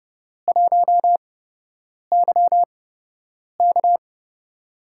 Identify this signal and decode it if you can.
Morse code, heard as a keyed tone, about 30 wpm, spaced as for 8 wpm, 700 Hz. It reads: 1YK
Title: USA Call Signs - 4 Characters - 8x Character Spacing - 30wpm